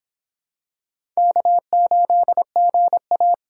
Morse code, heard as a keyed tone, about 24 wpm, 700 Hz. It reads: K8GA